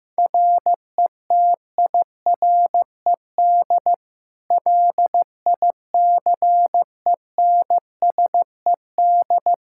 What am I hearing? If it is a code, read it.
Morse code, heard as a keyed tone, about 16 wpm, 700 Hz. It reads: RETIRED LICENSED
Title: Practice QSO - Exchange 3 - More Info - 15wpm